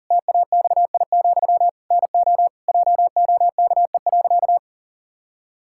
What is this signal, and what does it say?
Morse code, heard as a keyed tone, about 40 wpm, 700 Hz. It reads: TAXI, NO JOKE.
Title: Pangram Challenge v2 - 40wpm